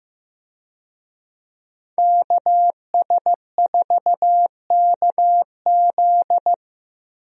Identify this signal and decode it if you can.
Morse code, heard as a keyed tone, about 15 wpm, 700 Hz. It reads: KS4KZ
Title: USA Call Signs - 5 Characters - 15wpm